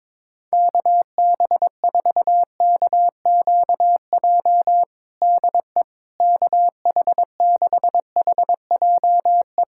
Morse code, heard as a keyed tone, about 22 wpm, 700 Hz. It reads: KB4KQJ DE KH6HJE
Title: USA Callsign DE Callsign 22wpm